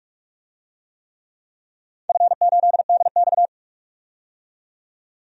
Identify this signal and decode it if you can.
Morse code, heard as a keyed tone, about 46 wpm, 700 Hz. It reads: F8DX
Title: DXCC-100 Call Signs - 4 Characters - 45wpm